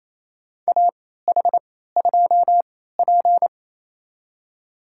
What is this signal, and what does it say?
Morse code, heard as a keyed tone, about 28 wpm, spaced as for 16 wpm, 700 Hz. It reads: AH2P
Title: USA Call Signs - 3x Character Spacing - 28wpm